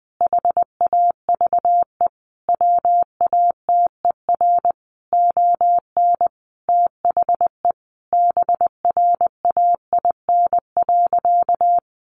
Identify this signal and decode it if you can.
Morse code, heard as a keyed tone, about 20 wpm, 700 Hz. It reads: HAVE WATER ON THE BRAIN.
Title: Sentences from Top 1100 Words 20wpm